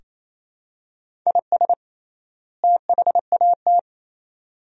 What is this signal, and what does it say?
Morse code, heard as a keyed tone, about 28 wpm, 700 Hz. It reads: IS THAT
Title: 3x Word-Spacing - Sentences from Top 100 Words - 28wpm